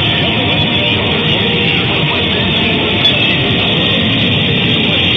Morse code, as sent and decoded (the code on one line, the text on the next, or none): none